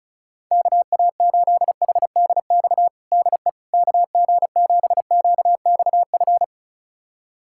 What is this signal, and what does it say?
Morse code, heard as a keyed tone, about 36 wpm, 700 Hz. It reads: KA8HDX DE KG7QXF
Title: USA Callsign DE Callsign - 6 Characters - 35wpm